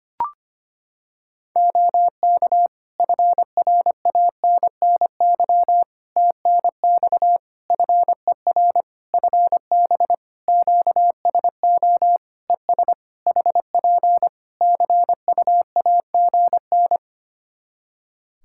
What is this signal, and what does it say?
Morse code, heard as a keyed tone, about 25 wpm, 700 Hz. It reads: OK FRANNY TNX FER FB QSO ES HP CUAGN
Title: Practice QSO - Exchange 4 - Ending - 25wpm